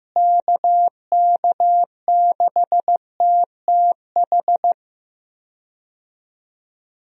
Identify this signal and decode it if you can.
Morse code, heard as a keyed tone, about 15 wpm, 700 Hz. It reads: KK6TTH
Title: USA Call Signs - 6 Characters - Repeated 2x - 15wpm